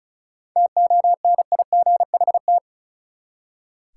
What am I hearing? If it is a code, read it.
Morse code, heard as a keyed tone, about 36 wpm, 700 Hz. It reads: TONIGHT
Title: Seven Letter Words - 35wpm